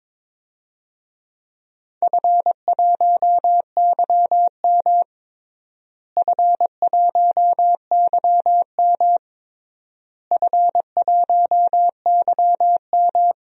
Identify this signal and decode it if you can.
Morse code, heard as a keyed tone, about 22 wpm, 700 Hz. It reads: F1YM F1YM F1YM
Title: DXCC-100 Call Signs - 4 Characters - Repeated 3x - 22wpm